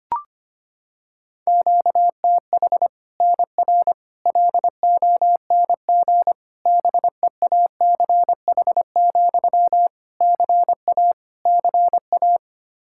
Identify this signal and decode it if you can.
Morse code, heard as a keyed tone, about 25 wpm, 700 Hz. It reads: QTH NR LONG BEACH, CA CA
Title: K1USN SST QSO Practice 25wpm